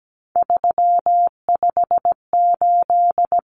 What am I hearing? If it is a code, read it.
Morse code, heard as a keyed tone, about 17 wpm, 700 Hz. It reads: 358
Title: Common Cut Numbers - 3-Digit 17wpm